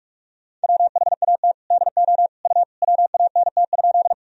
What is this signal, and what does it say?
Morse code, heard as a keyed tone, about 46 wpm, 700 Hz. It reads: WHAT DO U WANT?
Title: Sentences from Top 200 Words and Abbreviations 45wpm